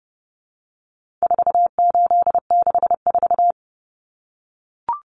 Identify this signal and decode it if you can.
Morse code, heard as a keyed tone, about 30 wpm, 700 Hz. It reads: 4864
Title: Numbers - 4-Digit 30wpm